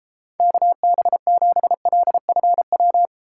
Morse code, heard as a keyed tone, about 31 wpm, 700 Hz. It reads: KB7LFW